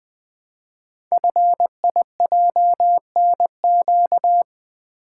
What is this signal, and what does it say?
Morse code, heard as a keyed tone, about 20 wpm, 700 Hz. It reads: FIJNQ
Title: Code Groups - 5 Letters - 20wpm